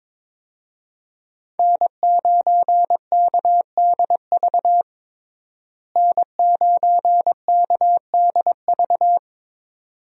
Morse code, heard as a keyed tone, about 22 wpm, 700 Hz. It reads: N9KDV N9KDV
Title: USA Call Signs - 5 Characters - Repeated 2x - 22wpm